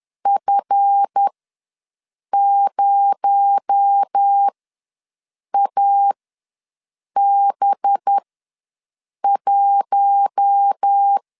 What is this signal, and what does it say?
Morse code, heard as a keyed tone, about 10 wpm, spaced as for 6 wpm, 800 Hz. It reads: F0AB1
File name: LEZIONI / 7 GF9Z / 7 GF9 ws 10wpm 4034